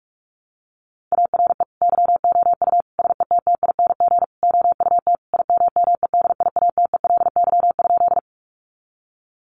Morse code, heard as a keyed tone, about 44 wpm, 700 Hz. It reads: ARE YOU SETTING OUT IMMEDIATELY?
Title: Sentences from Top 1300 Words 45wpm